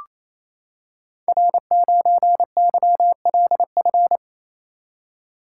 Morse code, heard as a keyed tone, about 27 wpm, 700 Hz. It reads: R9YLF
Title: Code Groups - 5 Letter:Numbers - 28wpm